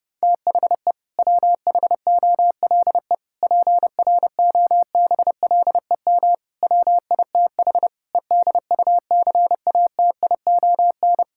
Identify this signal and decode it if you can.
Morse code, heard as a keyed tone, about 30 wpm, 700 Hz. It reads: THE WHOLE PROBLEM WITH EDUCATION
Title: Sentences from Top 1000 Words - Review Part 6 - 30wpm